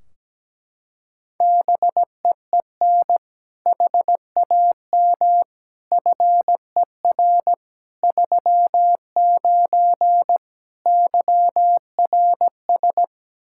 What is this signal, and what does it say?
Morse code, heard as a keyed tone, about 17 wpm, 700 Hz. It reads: BEEN HAM FER 39 YRS
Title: Practice QSO - Exchange 3 - More Info - 17wpm